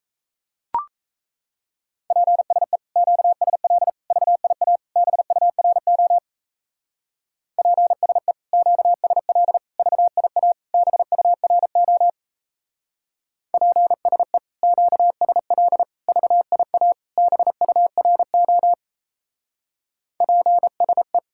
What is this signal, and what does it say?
Morse code, heard as a keyed tone, about 35 wpm, 700 Hz. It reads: PSE QSL VIA BURO PSE QSL VIA BURO PSE QSL VIA BURO PSE